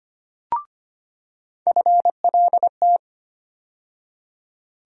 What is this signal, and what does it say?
Morse code, heard as a keyed tone, about 25 wpm, 700 Hz. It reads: FLT
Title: USA Call Sign Postfixes - 3 Letters 25wpm